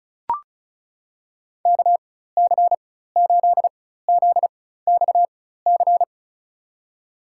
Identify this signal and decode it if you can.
Morse code, heard as a keyed tone, about 36 wpm, spaced as for 16 wpm, 700 Hz. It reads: KC8ZXC
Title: USA Call Signs - 4x Character Spacing - 35wpm